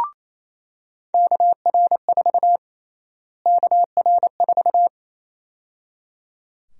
Morse code, heard as a keyed tone, about 27 wpm, 700 Hz. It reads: KR4 KR4
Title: USA Call Sign Prefixes - Repeated 2x - 28wpm